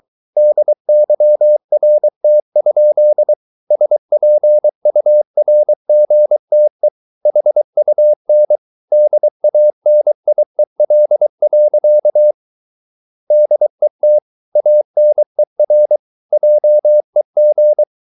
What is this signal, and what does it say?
Morse code, heard as a keyed tone, about 23 wpm, 600 Hz. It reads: DYRT? SPURGTE HUN DANIEL. DET ANER JEG
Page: Ferie 23 wpm